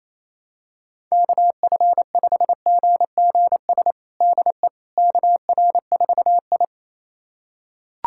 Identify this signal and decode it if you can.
Morse code, heard as a keyed tone, about 28 wpm, 700 Hz. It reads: KF5GGS DE KR4I
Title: USA Callsign DE Callsign 28wpm